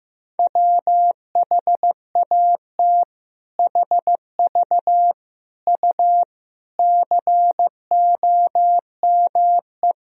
Morse code, heard as a keyed tone, about 15 wpm, 700 Hz. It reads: WHAT HV U COME
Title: Sentences from Top 100 Words and Abbreviations 15wpm